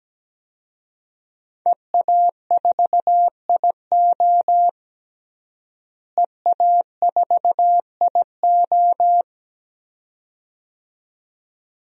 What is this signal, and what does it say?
Morse code, heard as a keyed tone, about 17 wpm, 700 Hz. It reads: EA4IO EA4IO
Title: DXCC-100 Call Signs - 5 Characters - Repeated 2x - 17wpm